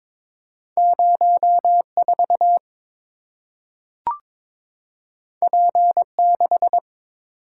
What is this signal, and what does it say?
Morse code, heard as a keyed tone, about 22 wpm, 700 Hz. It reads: 04 P6
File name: practice / 22wpm / Two Letter-Number Combinations - 22wpm